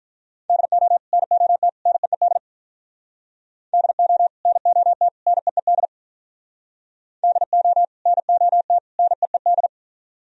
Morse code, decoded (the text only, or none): DO NOT NEED DO NOT NEED DO NOT NEED